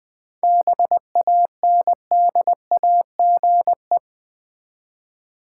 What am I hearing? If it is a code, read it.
Morse code, heard as a keyed tone, about 20 wpm, 700 Hz. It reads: BANDAGE
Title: Easy Words with Hints - 20wpm